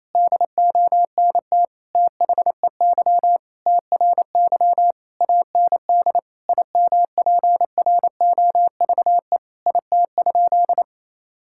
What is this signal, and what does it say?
Morse code, heard as a keyed tone, about 28 wpm, 700 Hz. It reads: DONT THEY TRY AND IMPROVE IT?
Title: Sentences from Top 1000 Words - Review Part 5 - 28wpm